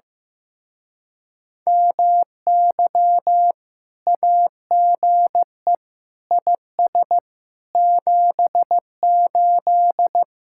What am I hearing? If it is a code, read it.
Morse code, heard as a keyed tone, about 15 wpm, 700 Hz. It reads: MY AGE IS 78
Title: Speed-Racing Common QSO Phrases Encore - 15wpm